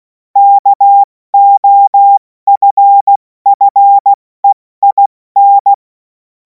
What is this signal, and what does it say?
Morse code, heard as a keyed tone, about 16 wpm, spaced as for 14 wpm, 800 Hz. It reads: KOFFEIN
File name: Morsekurs / Witze / Witz 11. August 2026 70 F80